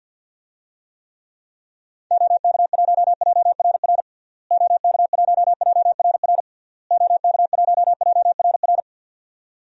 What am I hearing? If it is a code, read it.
Morse code, heard as a keyed tone, about 50 wpm, 700 Hz. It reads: OK1JRR OK1JRR OK1JRR
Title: DXCC-100 Call Signs - 6 Characters - Repeated 3x - 50wpm